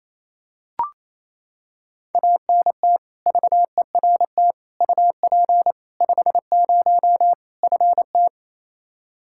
Morse code, heard as a keyed tone, about 28 wpm, 700 Hz. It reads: ANT VERT UP 50 FT
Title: Practice QSO - Exchange 2 - Info - 28wpm